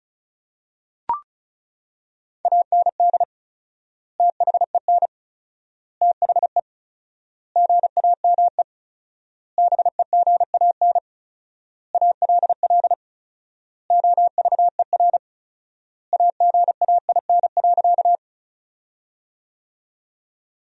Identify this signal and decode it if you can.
Morse code, heard as a keyed tone, about 34 wpm, 700 Hz. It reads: AND THEN THE GAME BEGAN ALL OVER AGAIN.
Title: 4x Word-Spacing - Sentences from Top 500 Words - 35wpm